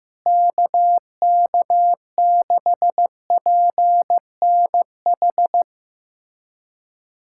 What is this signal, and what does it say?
Morse code, heard as a keyed tone, about 15 wpm, 700 Hz. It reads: KK6PNH
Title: USA Callsign DE Callsign - 6 Characters - 15wpm